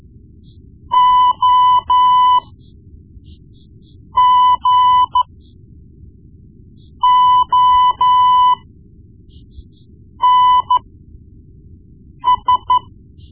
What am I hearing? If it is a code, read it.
Morse code, heard as a keyed tone, about 13 wpm, spaced as for 4 wpm, 1000 Hz. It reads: OGONS
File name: Morse Lessons / Morse Lessons / Tape-1B